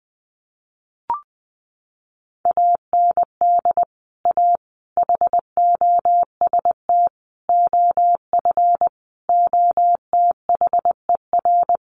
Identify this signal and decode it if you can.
Morse code, heard as a keyed tone, about 20 wpm, 700 Hz. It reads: AND A HOST OF OTHER
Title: Sets of 5 Words Encore2 20wpm